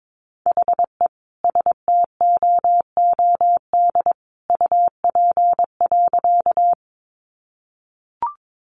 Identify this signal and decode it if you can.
Morse code, heard as a keyed tone, about 22 wpm, 700 Hz. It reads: HE STOOD UP.
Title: Sentences from Top 500 Words 22wpm